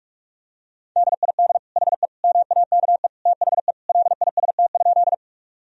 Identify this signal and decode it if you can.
Morse code, heard as a keyed tone, about 46 wpm, 700 Hz. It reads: DID HE MAKE THE LIST?
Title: Sentences from Top 1000 Words - Review Part 2 - 45wpm